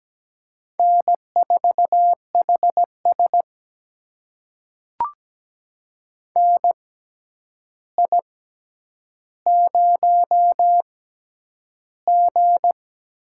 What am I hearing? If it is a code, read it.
Morse code, heard as a keyed tone, about 17 wpm, spaced as for 9 wpm, 700 Hz. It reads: N4HS NI0G
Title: USA Call Signs - 4 Characters - 6x Character Spacing - 17wpm